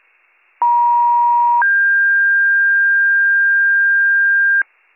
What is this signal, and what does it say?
Two-tone page, heard as a one-second tone, then a three-second tone, about 950 then 1600 Hz, with noise. It is heard from 0.6 to 4.6 s.